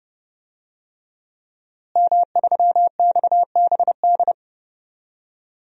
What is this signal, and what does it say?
Morse code, heard as a keyed tone, about 30 wpm, 700 Hz. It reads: M3XBD
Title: DXCC-100 Call Signs - 5 Characters - Repeated 3x - 30wpm